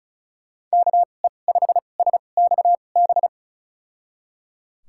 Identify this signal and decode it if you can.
Morse code, heard as a keyed tone, about 34 wpm, spaced as for 25 wpm, 700 Hz. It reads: KE5SXB
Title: USA Call Signs - 6 Characters - 2x Character Spacing - 35wpm